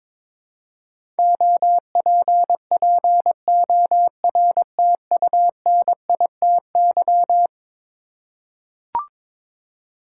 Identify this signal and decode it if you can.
Morse code, heard as a keyed tone, about 21 wpm, 700 Hz. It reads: OPPORTUNITY
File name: practice / 22wpm / Eleven Letter Words - 22wpm